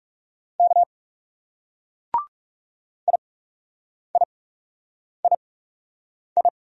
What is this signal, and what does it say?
Morse code, heard as a keyed tone, about 38 wpm, spaced as for 8 wpm, 700 Hz. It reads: K IIII